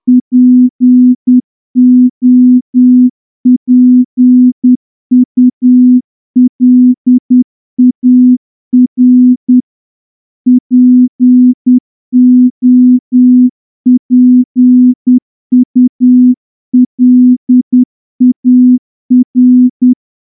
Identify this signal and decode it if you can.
Morse code, heard as a keyed tone, about 9 wpm, 250 Hz. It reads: POPULAR POPULAR
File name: morse-training-10WPM